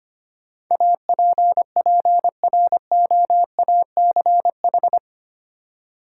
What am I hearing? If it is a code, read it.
Morse code, heard as a keyed tone, about 25 wpm, 700 Hz. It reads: APPROACH
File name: 8 Letter Words - First 3 Letters Given - 25wpm